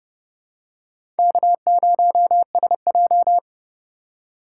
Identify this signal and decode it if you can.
Morse code, heard as a keyed tone, about 32 wpm, 700 Hz. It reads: K0SJ
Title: USA Call Signs - 4 Characters - 3x Character Spacing - 30wpm